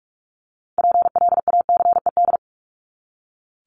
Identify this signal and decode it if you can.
Morse code, heard as a keyed tone, about 44 wpm, 700 Hz. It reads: PLACED
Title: ED Suffix - Top 100 Words with Base Words 45wpm